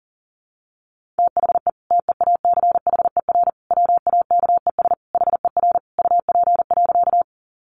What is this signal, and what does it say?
Morse code, heard as a keyed tone, about 40 wpm, 700 Hz. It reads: THE TEACHER WAKES HER UP.
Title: Easy Phrases 40wpm